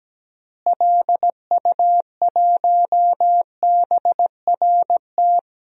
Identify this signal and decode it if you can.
Morse code, heard as a keyed tone, about 17 wpm, 700 Hz. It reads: LU1BRT
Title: DXCC-100 Call Signs - 6 Characters - Repeated 3x - 17wpm